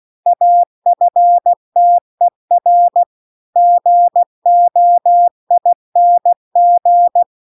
Morse code, heard as a keyed tone, about 16 wpm, 700 Hz. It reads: AFTER GOING